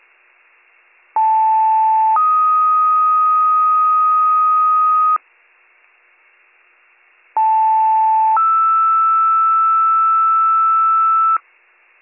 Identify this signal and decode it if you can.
Two-tone page, heard as a one-second tone, then a three-second tone, about 880 then 1250 Hz, about 880 then 1320 Hz. It is heard from 1.2 to 11.4 s.